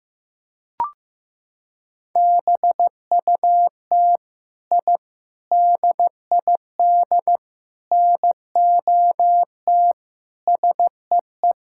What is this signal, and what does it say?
Morse code, heard as a keyed tone, about 15 wpm, 700 Hz. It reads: BUT I DID NOT SEE
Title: Sentences from Top 1000 Words - Review Part 5 - 15wpm